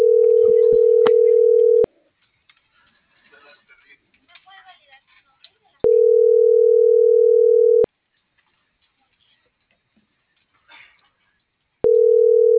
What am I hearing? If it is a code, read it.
Ringback tone (North America), 440 + 480 Hz.
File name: CENTRAL MORENA / out ram.lmorena.2025